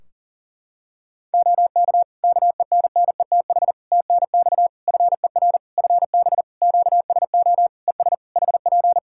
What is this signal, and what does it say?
Morse code, heard as a keyed tone, about 40 wpm, 700 Hz. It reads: OK KENNETH TNX FER FB QSO ES HP
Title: Practice QSO - Exchange 4 - Ending - 40wpm